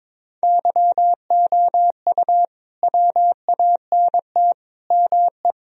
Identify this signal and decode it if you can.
Morse code, heard as a keyed tone, about 22 wpm, 700 Hz. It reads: YOU WANT ME
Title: Sentences from Top 1000 Words - Review Part 2 - 22wpm